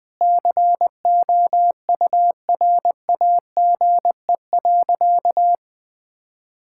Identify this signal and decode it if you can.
Morse code, heard as a keyed tone, about 21 wpm, 700 Hz. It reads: COURAGE.
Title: Sentences with 4-Word Phrases - 20wpm